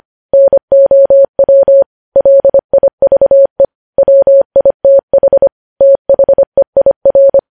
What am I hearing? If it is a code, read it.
Morse code, heard as a keyed tone, about 25 wpm, 550 Hz. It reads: NOW LIVE WITH THEIR
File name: podcast / News headlines-1x-25x25